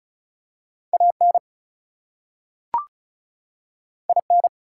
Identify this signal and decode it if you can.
Morse code, heard as a keyed tone, about 36 wpm, 700 Hz. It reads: AN IN
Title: Two Letter Words - 35wpm